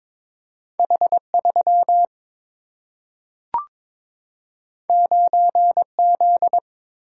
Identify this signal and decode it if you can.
Morse code, heard as a keyed tone, about 22 wpm, 700 Hz. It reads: H3 9Z
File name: practice / 22wpm / Two Letter-Number Combinations - 22wpm